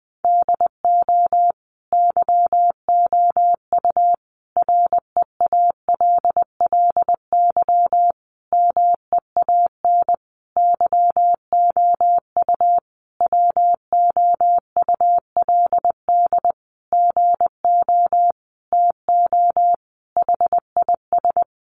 Morse code, heard as a keyed tone, about 20 wpm, 700 Hz. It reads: DO YOU REALLY MEAN YOU WOULD GO TO HIS
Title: Sentences from Top 600 Words 20wpm